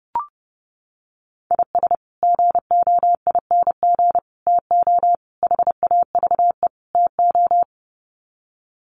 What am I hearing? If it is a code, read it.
Morse code, heard as a keyed tone, about 30 wpm, 700 Hz. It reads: IS GOING TO HAVE TO
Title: Sets of 5 Words 30wpm